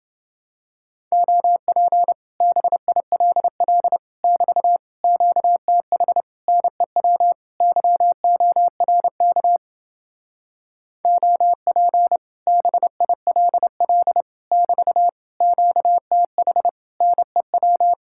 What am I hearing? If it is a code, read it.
Morse code, heard as a keyed tone, about 28 wpm, 700 Hz. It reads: OP BILL <BT> QTH NEW YORK OP BILL <BT> QTH NEW